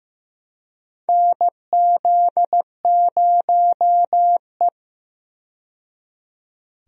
Morse code, heard as a keyed tone, about 15 wpm, 700 Hz. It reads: NZ0E
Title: USA Call Signs - 4 Characters - 15wpm